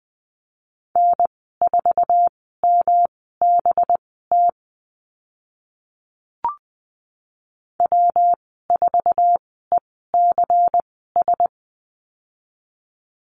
Morse code, heard as a keyed tone, about 20 wpm, spaced as for 15 wpm, 700 Hz. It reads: N4MBT W4ECS
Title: 186 Lesson 30g - Letter b - Call Signs 20wpm